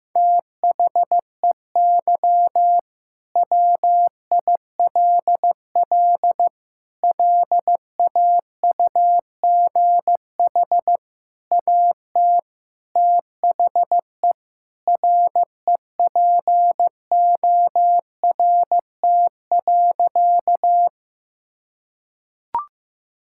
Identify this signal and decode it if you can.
Morse code, heard as a keyed tone, about 15 wpm, 700 Hz. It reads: THEY WILL LAUGH AT THE REPORT.
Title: Sentences from Top 1000 Words - Review Part 6 - 15wpm